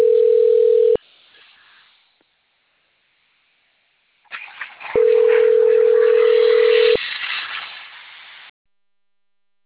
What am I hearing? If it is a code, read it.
Ringback tone (North America), 440 + 480 Hz.